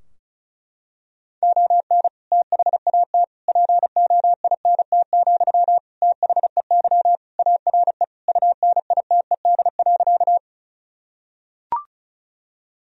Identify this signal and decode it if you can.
Morse code, heard as a keyed tone, about 36 wpm, 700 Hz. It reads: ON THAT POINT, THEY ARE UNITED.
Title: Sentences from Top 1000 Words - Review Part 4 - 35wpm